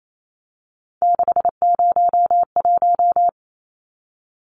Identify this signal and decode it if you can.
Morse code, heard as a keyed tone, about 28 wpm, 700 Hz. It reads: 601
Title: Numbers - 3-Digit 28wpm